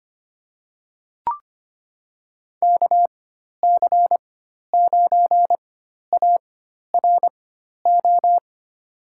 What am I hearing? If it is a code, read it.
Morse code, heard as a keyed tone, about 25 wpm, spaced as for 12 wpm, 700 Hz. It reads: KC9ARO
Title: USA Call Signs - 6 Characters - 4x Character Spacing - 25wpm